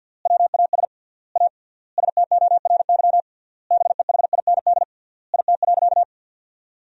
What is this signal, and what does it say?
Morse code, heard as a keyed tone, about 50 wpm, 700 Hz. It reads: WAS A STORY BEHIND IT.
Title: 3x Word-Spacing - Sentences from Top 500 Words - 50wpm